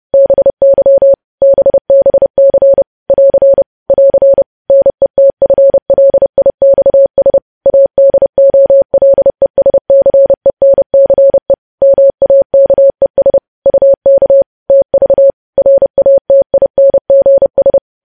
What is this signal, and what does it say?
Morse code, heard as a keyed tone, about 30 wpm, 550 Hz. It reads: BY BBC <AR> <AR> NETFLIXS ADOLESCENCE MAKES UK TV RATINGS
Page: news headlines in morse code at 30 wpm